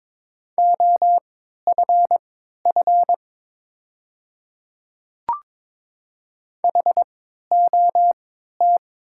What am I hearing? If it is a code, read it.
Morse code, heard as a keyed tone, about 22 wpm, spaced as for 13 wpm, 690 Hz. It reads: OFF HOT
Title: 3x Character Spacing - Top 300 Words - 22wpm